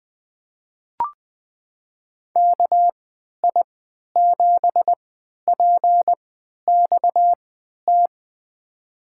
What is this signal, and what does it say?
Morse code, heard as a keyed tone, about 20 wpm, spaced as for 11 wpm, 700 Hz. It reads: KI7PXT
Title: USA Call Signs - 6 Characters - 3x Character Spacing - 20wpm